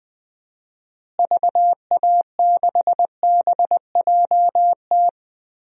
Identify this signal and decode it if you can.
Morse code, heard as a keyed tone, about 20 wpm, 700 Hz. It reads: VA6BJT
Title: DXCC-100 Call Signs - 6 Characters - Repeated 2x - 20wpm